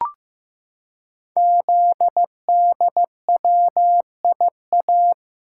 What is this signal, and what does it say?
Morse code, heard as a keyed tone, about 15 wpm, 700 Hz. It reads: ZDWIA
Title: Code Groups - 5 Letters - 15wpm